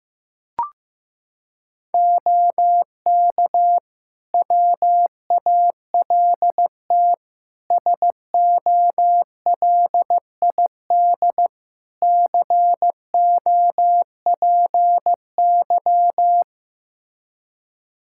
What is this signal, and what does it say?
Morse code, heard as a keyed tone, about 15 wpm, 700 Hz. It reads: OK WALT SOLID COPY
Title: Practice QSO - Exchange 3 - More Info - 15wpm